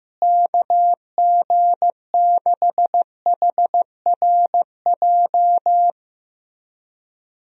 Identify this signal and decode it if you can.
Morse code, heard as a keyed tone, about 16 wpm, 700 Hz. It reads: KG6HRJ
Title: USA Call Signs - 6 Characters - Repeated 3x - 15wpm